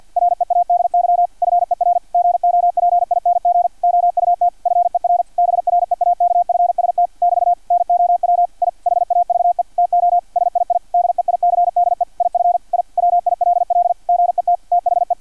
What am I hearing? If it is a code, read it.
Morse code, heard as a keyed tone, about 50 wpm, 700 Hz, with radio noise. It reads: READY FER GOPING OUT FER BREAKFST <BT> NOW I HAVE TO SII DEICDE IF I WILL GET THE
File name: archived / CRSnet